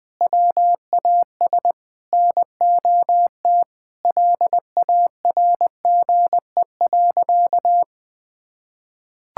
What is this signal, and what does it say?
Morse code, heard as a keyed tone, about 21 wpm, 700 Hz. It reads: WAS NOT LARGE.
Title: Sentences from Top 1000 Words - Review Part 5 - 20wpm